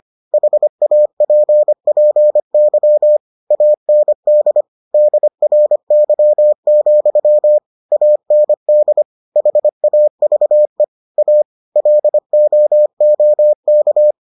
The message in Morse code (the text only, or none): HAPPY AND DRY, AND HAVE A LOOK